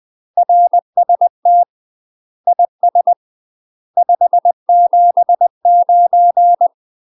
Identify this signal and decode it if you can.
Morse code, heard as a keyed tone, about 20 wpm, 700 Hz. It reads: RST IS 579
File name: UK MASTER QSOs TRAINING / 20WPM / UK MASTER QSOs 0052